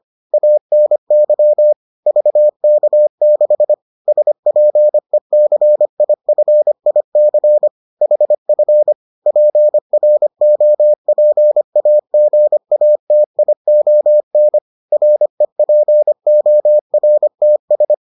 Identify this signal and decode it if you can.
Morse code, heard as a keyed tone, about 25 wpm, 600 Hz. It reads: ANY VK6 SPECIFIC HF PROPAGATION REPORTS